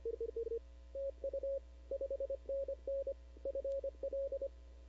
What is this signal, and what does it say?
Morse code, heard as a keyed tone, about 25 wpm, 560 Hz. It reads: TU 5NN FL
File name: CW / 2005-k5tr arrl cw dx -19:00